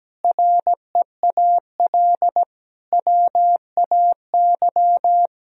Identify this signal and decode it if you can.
Morse code, heard as a keyed tone, about 17 wpm, 700 Hz. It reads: REAL WAY
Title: Sentences from Top 1000 Words - Review Part 2 - 17wpm